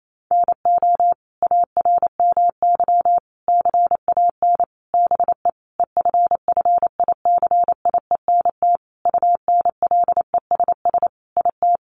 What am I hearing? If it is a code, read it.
Morse code, heard as a keyed tone, about 28 wpm, 700 Hz. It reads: NO ARMY CAN BE EFFICIENT UNLESS IT
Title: Sentences from Top 1300 Words 28wpm